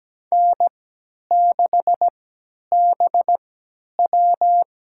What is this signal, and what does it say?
Morse code, heard as a keyed tone, about 17 wpm, spaced as for 10 wpm, 700 Hz. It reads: N6BW
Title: USA Call Signs - 3x Character Spacing - 17wpm